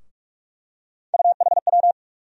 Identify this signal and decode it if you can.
Morse code, heard as a keyed tone, about 47 wpm, 700 Hz. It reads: UHW
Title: USA Call Sign Postfixes - 3 Letters 45wpm